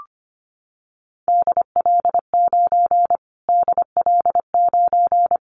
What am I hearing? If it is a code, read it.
Morse code, heard as a keyed tone, about 25 wpm, 700 Hz. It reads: DL9 DL9
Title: Callsign Prefixes for Top 25 DXCC - Repeated 3x - 25wpm